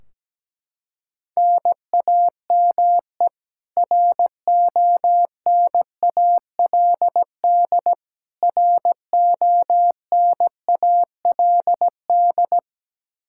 Morse code, heard as a keyed tone, about 17 wpm, 700 Hz. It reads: NAME RONALD RONALD
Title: Practice QSO - Exchange 1 - Beginning - 17wpm